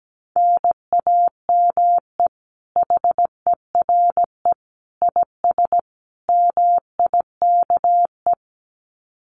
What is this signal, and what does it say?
Morse code, heard as a keyed tone, about 17 wpm, 700 Hz. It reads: NAME HERE IS MIKE
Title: Common QSO Phrases 17wpm